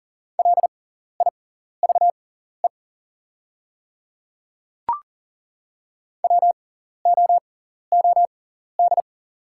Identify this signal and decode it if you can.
Morse code, heard as a keyed tone, about 40 wpm, spaced as for 14 wpm, 700 Hz. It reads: LIVE WOOD
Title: 6x Character Spacing - Top 300 Words - 40wpm